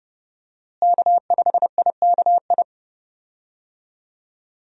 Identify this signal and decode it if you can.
Morse code, heard as a keyed tone, about 32 wpm, 700 Hz. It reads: K5IKI
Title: USA Call Signs - 5 Characters - Repeated 3x - 30wpm